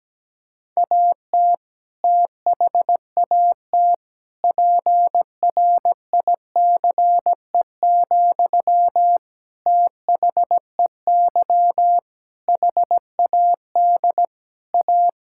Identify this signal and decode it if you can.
Morse code, heard as a keyed tone, about 17 wpm, 700 Hz. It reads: AT THAT PRICE, THEY HAD A
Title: Sentences from Top 1000 Words - Review Part 5 - 17wpm